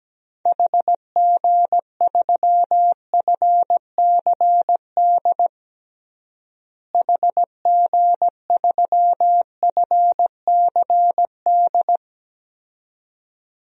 Morse code, heard as a keyed tone, about 17 wpm, 700 Hz. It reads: HG3FCD HG3FCD
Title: DXCC-100 Call Signs - 6 Characters - Repeated 2x - 17wpm